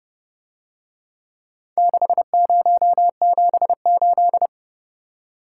Morse code, heard as a keyed tone, about 29 wpm, 700 Hz. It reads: 6078